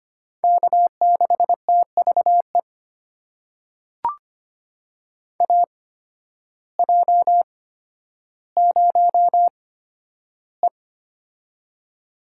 Morse code, heard as a keyed tone, about 25 wpm, spaced as for 12 wpm, 700 Hz. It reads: K6TVE AJ0E
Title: USA Call Signs - 8x Character Spacing - 25wpm